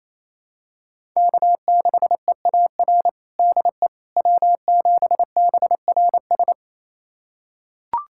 Morse code, heard as a keyed tone, about 28 wpm, 700 Hz. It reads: K6EAR DE W7BRS
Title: USA Callsign DE Callsign - 5 Characters - 28wpm